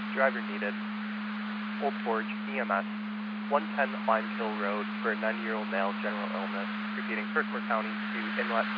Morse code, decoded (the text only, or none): none